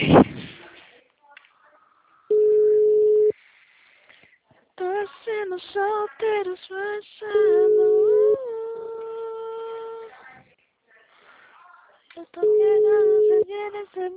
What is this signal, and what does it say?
Ringback tone (Europe, 425 Hz), with speech.
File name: out- -2022- - - .48687